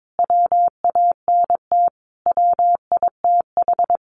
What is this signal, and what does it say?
Morse code, heard as a keyed tone, about 22 wpm, 700 Hz. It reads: WANT WITH
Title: Sentences from Top 200 Words 22wpm